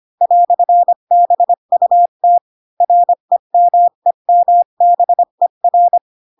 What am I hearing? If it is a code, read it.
Morse code, heard as a keyed tone, about 25 wpm, 700 Hz. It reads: "BUT REMEMBER
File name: Pinocchio / Pinocchio ch8 20 25 ik1hge